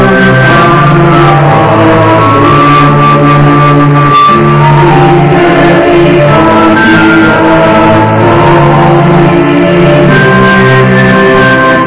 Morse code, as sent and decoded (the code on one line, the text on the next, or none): none